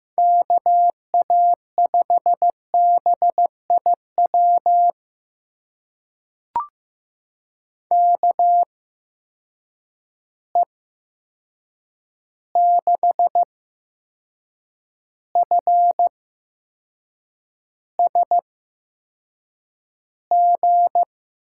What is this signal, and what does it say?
Morse code, heard as a keyed tone, about 15 wpm, spaced as for 6 wpm, 700 Hz. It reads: KA5BIW KE6FSG
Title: USA Call Signs - 6 Characters - 8x Character Spacing - 15wpm